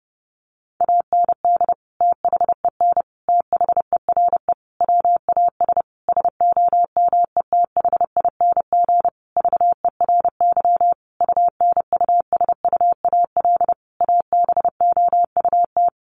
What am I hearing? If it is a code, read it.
Morse code, heard as a keyed tone, about 30 wpm, 700 Hz. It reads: AND THEN THERE WAS SOMETHING VERY UNUSUAL ABOUT
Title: Sentences from Top 1400 Words 30wpm